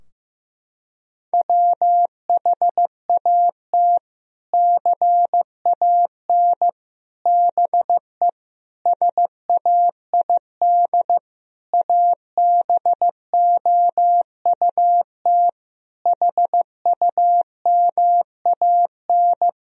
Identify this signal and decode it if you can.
Morse code, heard as a keyed tone, about 16 wpm, 700 Hz. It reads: WHAT CAN BE SAID ABOUT HUMAN
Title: Sentences from Top 1000 Words - Review Part 4 - 15wpm